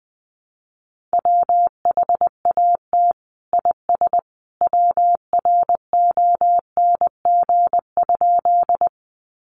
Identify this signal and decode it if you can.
Morse code, heard as a keyed tone, about 20 wpm, 700 Hz. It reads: WHAT IS WRONG?
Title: Sentences from Top 900 Words 20wpm